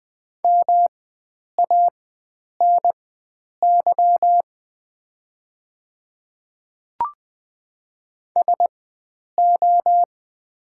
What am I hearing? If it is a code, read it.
Morse code, heard as a keyed tone, about 20 wpm, spaced as for 9 wpm, 700 Hz. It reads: MANY SO
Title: 4x Character Spacing - Top 100 Words - 20wpm